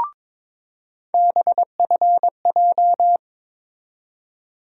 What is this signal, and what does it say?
Morse code, heard as a keyed tone, about 22 wpm, 700 Hz. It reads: BFJ
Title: USA Call Sign Postfixes - 3 Letters 22wpm